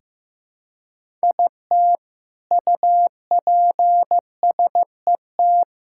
Morse code, heard as a keyed tone, about 15 wpm, 700 Hz. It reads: IT UPSET
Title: Sentences from Top 1000 Words - Review Part 5 - 15wpm